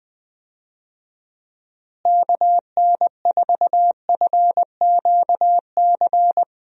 Morse code, heard as a keyed tone, about 20 wpm, 700 Hz. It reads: KN4FQC
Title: USA Call Signs - 6 Characters - Repeated 2x - 20wpm